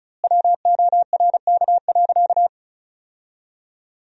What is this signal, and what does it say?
Morse code, heard as a keyed tone, about 36 wpm, 700 Hz. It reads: WORK.
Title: 2x Word-Spacing - Sentences from Top 300 Words - 35wpm